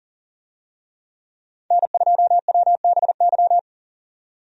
Morse code, heard as a keyed tone, about 40 wpm, 700 Hz. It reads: N2WBY
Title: USA Call Signs - 5 Characters - 40wpm